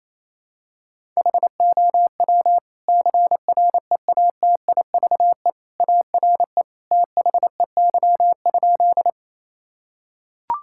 Morse code, heard as a keyed tone, about 28 wpm, 700 Hz. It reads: HOW CREATIVE ARE THEY?
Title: Sentences from Top 1000 Words - Review Part 5 - 28wpm